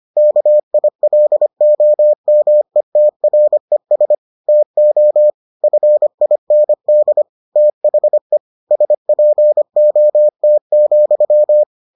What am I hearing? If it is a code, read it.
Morse code, heard as a keyed tone, about 25 wpm, 600 Hz. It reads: KILOMETRES TO FIND THE SPOT,